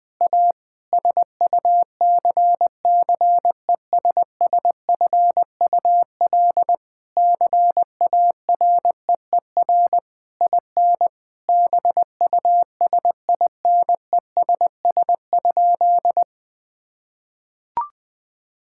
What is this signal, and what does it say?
Morse code, heard as a keyed tone, about 21 wpm, 700 Hz. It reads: A SUCCESSFUL CAREER IN BUSINESS?
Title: Sentences from Top 1000 Words - Review Part 6 - 20wpm